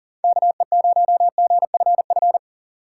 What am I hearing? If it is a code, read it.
Morse code, heard as a keyed tone, about 40 wpm, 700 Hz. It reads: KE0GFF
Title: USA Call Signs - 6 Characters - 3x Character Spacing - 40wpm